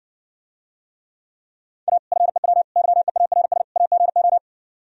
Morse code, heard as a keyed tone, about 60 wpm, 700 Hz. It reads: A FEW YEARS AGO